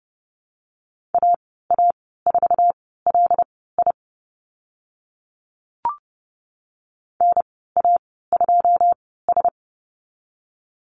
Morse code, heard as a keyed tone, about 30 wpm, spaced as for 17 wpm, 700 Hz. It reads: AA4LI NA2S
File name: Lesson 14g - Number 2 - Call Signs 30wpm